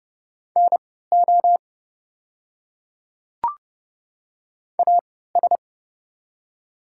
Morse code, heard as a keyed tone, about 30 wpm, spaced as for 17 wpm, 700 Hz. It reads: NO AS
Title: 3x Character Spacing - Top 100 Words - 30wpm